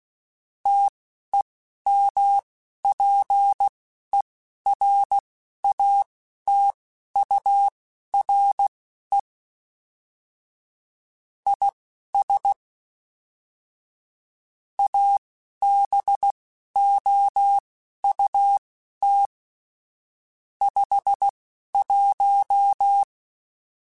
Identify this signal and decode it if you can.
Morse code, heard as a keyed tone, about 16 wpm, spaced as for 11 wpm, 800 Hz. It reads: TEMPERATURE IS ABOUT 51
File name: Weather Sets / Weather Sets-mp3s / Weather 4 21 @10